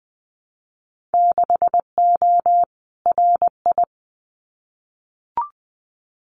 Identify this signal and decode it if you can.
Morse code, heard as a keyed tone, about 19 wpm, 700 Hz. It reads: 6O RI
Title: Winter Field Day Exchanges Encore 20wpm